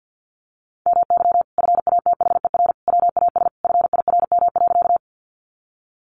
Morse code, heard as a keyed tone, about 50 wpm, 700 Hz. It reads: MY FATHER WAS FIRM.
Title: Sentences from Top 1400 Words 50wpm